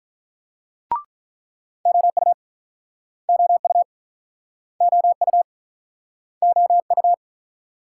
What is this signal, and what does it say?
Morse code, heard as a keyed tone, about 43 wpm, 700 Hz. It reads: OU OU OU OU
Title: Speed-Racing Multi-letter Phonemes - 35wpm